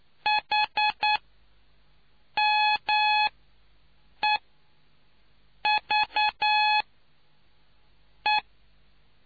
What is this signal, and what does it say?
Morse code, heard as a keyed tone, about 9 wpm, spaced as for 5 wpm, 850 Hz. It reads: HMEVE